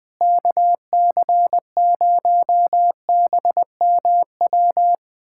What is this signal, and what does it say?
Morse code, heard as a keyed tone, about 21 wpm, 700 Hz. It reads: KC0BMW
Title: USA Callsign DE Callsign - 6 Characters - 20wpm